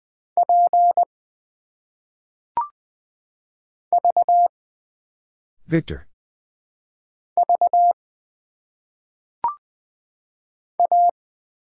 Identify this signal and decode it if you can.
Morse code, heard as a keyed tone, about 20 wpm, 700 Hz. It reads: P V V A
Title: Single Letters - Phonetic Pronunciation 20wpm